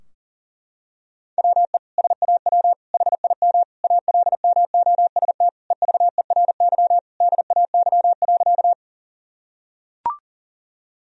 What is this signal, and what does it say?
Morse code, heard as a keyed tone, about 37 wpm, 700 Hz. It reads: WE SAW HIM ALMOST EVERY DAY.
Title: Sentences from Top 1000 Words - Review Part 4 - 40wpm